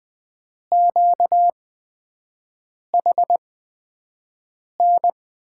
Morse code, heard as a keyed tone, about 20 wpm, spaced as for 5 wpm, 700 Hz. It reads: QHN
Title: USA Call Signs - 8x Character Spacing - 20wpm